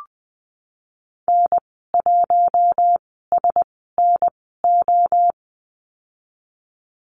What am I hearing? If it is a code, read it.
Morse code, heard as a keyed tone, about 19 wpm, spaced as for 14 wpm, 700 Hz. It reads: N1SNO
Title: 076 Lesson 13g - Letter L - Call Signs 20wpm